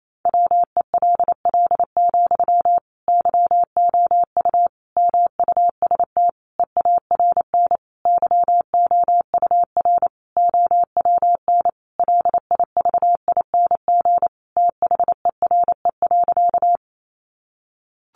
Morse code, heard as a keyed tone, about 28 wpm, 700 Hz. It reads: WELL, YOU MUST EARN YOUR OWN LIVING THERE.
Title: Sentences from Top 1500 Words 28wpm